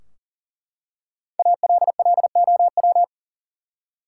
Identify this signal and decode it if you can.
Morse code, heard as a keyed tone, about 40 wpm, 700 Hz. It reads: ALLOW
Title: Five Letter Words - 40wpm